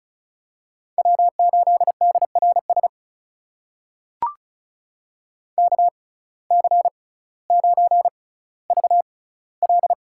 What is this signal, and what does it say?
Morse code, heard as a keyed tone, about 36 wpm, spaced as for 18 wpm, 700 Hz. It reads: W8DRS KC9VL
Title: USA Call Signs - 5 Characters - 6x Character Spacing - 35wpm